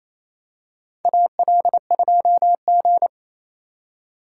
Morse code, heard as a keyed tone, about 28 wpm, 700 Hz. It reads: AL2G
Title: USA Call Signs - 4 Characters - 4x Character Spacing - 28wpm